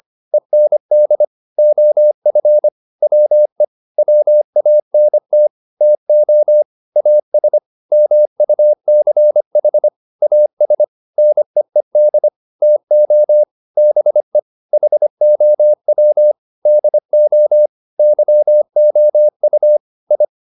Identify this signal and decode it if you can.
Morse code, heard as a keyed tone, about 25 wpm, 600 Hz. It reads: END OF WE WANT TO AS MUCH AS NEED TO BE HOW DO YOU I